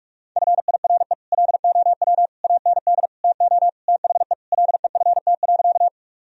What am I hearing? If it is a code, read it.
Morse code, heard as a keyed tone, about 46 wpm, 700 Hz. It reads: FIRE LOW AND TO THE LEFT.
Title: Sentences from Top 1000 Words - Review Part 1 - 45wpm